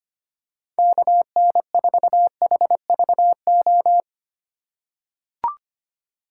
Morse code, heard as a keyed tone, about 25 wpm, 700 Hz. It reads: KN4HVO
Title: USA Call Signs - 4x Character Spacing - 25wpm